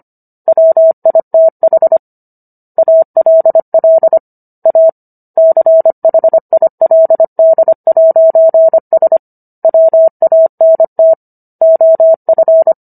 Morse code, heard as a keyed tone, about 25 wpm, 650 Hz. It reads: WITH ALL A CHILD'S WANT OF